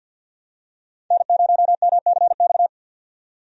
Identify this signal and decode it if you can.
Morse code, heard as a keyed tone, about 46 wpm, 700 Hz. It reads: N0MCX
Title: USA Call Signs - 5 Characters - 50wpm